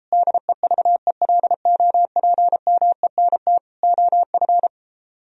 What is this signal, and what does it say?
Morse code, heard as a keyed tone, about 33 wpm, 700 Hz. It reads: DEVELOPMENT OF